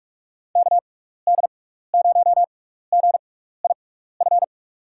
Morse code, heard as a keyed tone, about 44 wpm, spaced as for 15 wpm, 700 Hz. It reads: KD0GIF
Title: USA Call Signs - 6x Character Spacing - 45wpm